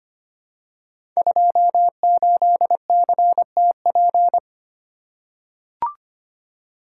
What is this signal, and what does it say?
Morse code, heard as a keyed tone, about 25 wpm, 700 Hz. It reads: 28CTP
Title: Code Groups - 5 Letter:Numbers - 25wpm